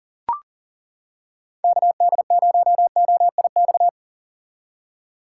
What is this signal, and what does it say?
Morse code, heard as a keyed tone, about 40 wpm, 700 Hz. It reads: KD0OIX
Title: K1USN SST QSO Practice - 40wpm